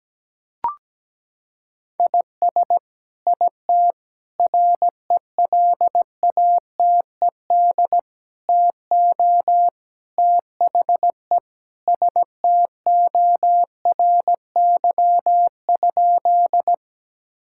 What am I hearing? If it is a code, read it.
Morse code, heard as a keyed tone, about 17 wpm, 700 Hz. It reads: IS IT RELATED TO THE STORY?
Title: Sentences from Top 1000 Words - Review Part 5 - 17wpm